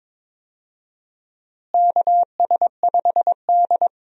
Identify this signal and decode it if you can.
Morse code, heard as a keyed tone, about 21 wpm, 700 Hz. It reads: KS5D
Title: USA Call Signs - 4 Characters - Repeated 3x - 22wpm